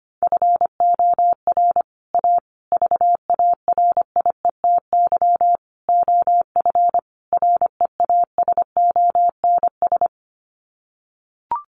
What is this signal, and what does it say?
Morse code, heard as a keyed tone, about 25 wpm, 700 Hz. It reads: FOR A VARIETY OF REASONS
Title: Sets of 5 Words Encore 25wpm